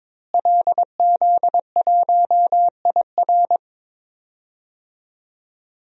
Morse code, heard as a keyed tone, about 22 wpm, 700 Hz. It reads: LZ1IR
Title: DXCC-100 Call Signs - 5 Characters - Repeated 3x - 22wpm